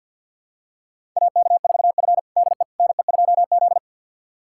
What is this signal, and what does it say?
Morse code, heard as a keyed tone, about 49 wpm, 700 Hz. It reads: AK4F DE NE2Z
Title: USA Callsign DE Callsign - 4 Characters - 50wpm